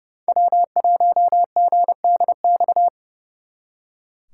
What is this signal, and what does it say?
Morse code, heard as a keyed tone, about 36 wpm, 700 Hz. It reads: W1GDX
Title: Speed Racing - USA 1x3 Call Signs 30wpm